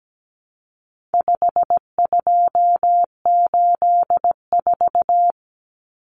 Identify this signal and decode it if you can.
Morse code, heard as a keyed tone, about 17 wpm, 700 Hz. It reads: 5284
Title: Common Cut Numbers - 4-Digit 17wpm